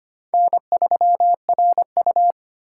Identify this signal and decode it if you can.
Morse code, heard as a keyed tone, about 25 wpm, 700 Hz. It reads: N3RU
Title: USA Call Signs - 4 Characters - 2x Character Spacing - 25wpm